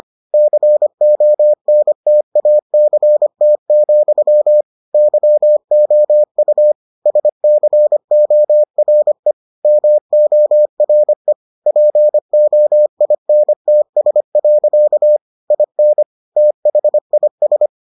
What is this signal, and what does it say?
Morse code, heard as a keyed tone, about 25 wpm, 600 Hz. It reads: CONTACT, YOU SCORE MORE POINTS. IN THIS